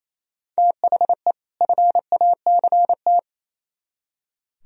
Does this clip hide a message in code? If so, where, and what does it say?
Yes: Morse code from 0.6 to 3.2 s, heard as a keyed tone, about 36 wpm, 700 Hz. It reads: THE FACT